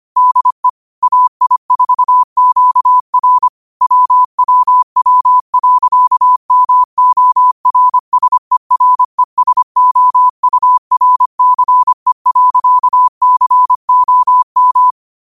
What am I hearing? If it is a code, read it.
Morse code, heard as a keyed tone, about 25 wpm, 1000 Hz. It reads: DE AI4QR WWW.MORSERESOURCE.COM
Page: New quotes every day in morse code at 25 Words per minute.